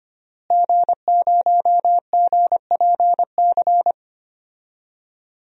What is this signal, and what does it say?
Morse code, heard as a keyed tone, about 25 wpm, 700 Hz. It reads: G0GPC
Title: DXCC-100 Call Signs - 5 Characters - Repeated 3x - 25wpm